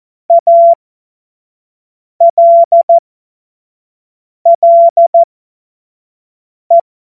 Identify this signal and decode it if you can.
Morse code, heard as a keyed tone, about 14 wpm, spaced as for 5 wpm, 680 Hz. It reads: ALLE
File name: Morsekurs / Witze / Witz 04. April 2023 25 F70